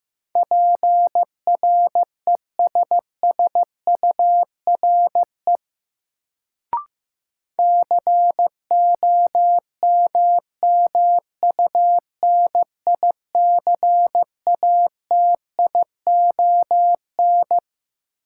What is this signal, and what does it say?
Morse code, heard as a keyed tone, about 15 wpm, 700 Hz. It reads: PRESSURE COMMUNICATION
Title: Sentences from Top 1000 Words - Review Part 6 - New Word Drill - 15wpm